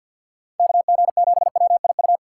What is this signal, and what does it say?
Morse code, heard as a keyed tone, about 47 wpm, 700 Hz. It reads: KG7WIU
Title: USA Call Signs - 6 Characters - 3x Character Spacing - 50wpm